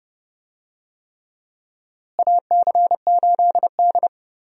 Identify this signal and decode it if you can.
Morse code, heard as a keyed tone, about 30 wpm, 700 Hz. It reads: AC8D
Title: USA Call Signs - 4 Characters - Repeated 3x - 30wpm